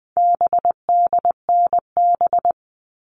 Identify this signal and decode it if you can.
Morse code, heard as a keyed tone, about 20 wpm, 700 Hz. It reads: BDNB
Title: All Cut Numbers - 4-Digit 20wpm